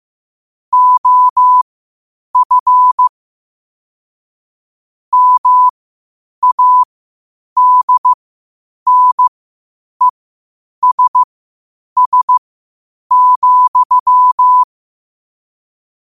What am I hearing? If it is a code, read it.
Morse code, heard as a keyed tone, about 15 wpm, spaced as for 8 wpm, 1000 Hz. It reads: OF MADNESS,
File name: qotdfiles / 8wpm